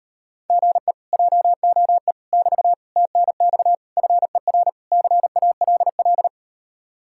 Morse code, heard as a keyed tone, about 39 wpm, 700 Hz. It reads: GE JOE <BT> TNX FER CALL